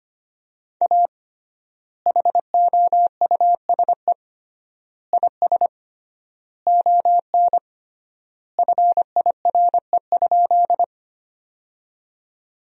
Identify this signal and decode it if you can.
Morse code, heard as a keyed tone, about 25 wpm, 700 Hz. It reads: A HOUSE IS ON FIRE?
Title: 3x Word-Spacing - Sentences from Top 400 Words - 25wpm